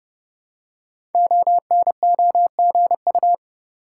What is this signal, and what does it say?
Morse code, heard as a keyed tone, about 28 wpm, 700 Hz. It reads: ONOGU
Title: Code Groups - 5 Letters - 30wpm